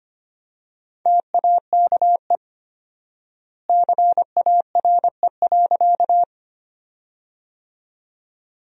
Morse code, heard as a keyed tone, about 25 wpm, 700 Hz. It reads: TAKE CARE.
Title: 4x Word-Spacing - Sentences from Top 300 Words - 25wpm